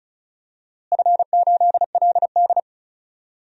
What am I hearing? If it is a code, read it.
Morse code, heard as a keyed tone, about 36 wpm, 690 Hz. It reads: F8LD
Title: DXCC-100 Call Signs - 4 Characters - Repeated 2x - 35wpm